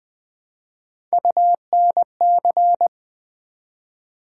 Morse code, heard as a keyed tone, about 20 wpm, 700 Hz. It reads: UNC
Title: Build A Word - 1 Letter At A Time - 5-Letter Words - 20wpm